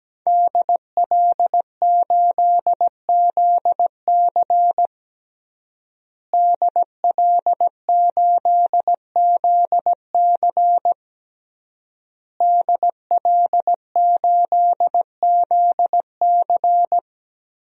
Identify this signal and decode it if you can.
Morse code, heard as a keyed tone, about 17 wpm, 700 Hz. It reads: DL8ZC DL8ZC DL8ZC
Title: DXCC-100 Call Signs - 5 Characters - Repeated 3x - 17wpm